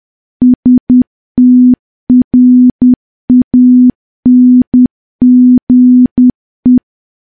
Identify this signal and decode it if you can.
Morse code, heard as a keyed tone, about 10 wpm, 250 Hz. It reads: STRANGE
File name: morse / morse-training-10WPM